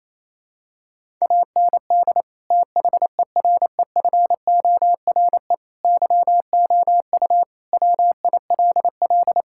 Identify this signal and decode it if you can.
Morse code, heard as a keyed tone, about 27 wpm, 700 Hz. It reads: AND THEREFORE YOU WILL
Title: Sentences from Top 1000 Words - Review Part 5 - 28wpm